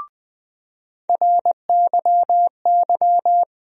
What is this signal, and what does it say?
Morse code, heard as a keyed tone, about 20 wpm, 700 Hz. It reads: RYY
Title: USA Call Sign Postfixes - 3 Letters 20wpm